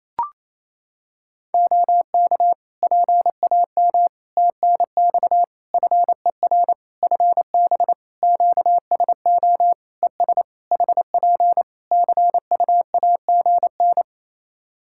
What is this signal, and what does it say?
Morse code, heard as a keyed tone, about 28 wpm, 700 Hz. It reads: OK PAM TNX FER FB QSO ES HP CUAGN
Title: K1USN SST QSO Practice 28wpm